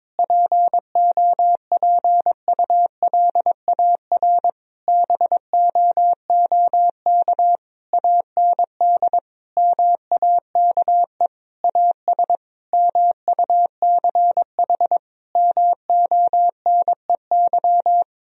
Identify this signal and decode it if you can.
Morse code, heard as a keyed tone, about 22 wpm, 700 Hz. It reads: POPULAR BOOK AND MAKE AS MUCH MONEY
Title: Sentences from Top 1000 Words - Review Part 3 - 22wpm